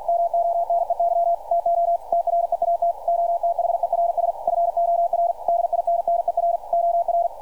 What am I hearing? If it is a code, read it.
Morse code, heard as a keyed tone, about 49 wpm, 680 Hz, with radio noise. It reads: MONEY TO TREAT OTHERS FOR DINNER OR
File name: archived / CRSnet